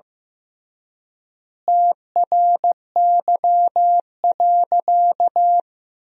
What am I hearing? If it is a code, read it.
Morse code, heard as a keyed tone, about 15 wpm, 700 Hz. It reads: TRY.
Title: 3x Word-Spacing - Sentences from Top 200 Words - 15wpm